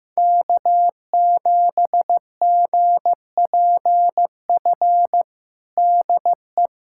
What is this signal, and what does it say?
Morse code, heard as a keyed tone, about 15 wpm, 700 Hz. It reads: K7GPF DE
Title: USA Callsign DE Callsign - 5 Characters - 15wpm